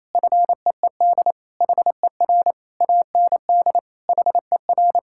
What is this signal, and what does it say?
Morse code, heard as a keyed tone, about 28 wpm, 700 Hz. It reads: FEED HER AND HER
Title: Sentences from Top 1600 Words 28wpm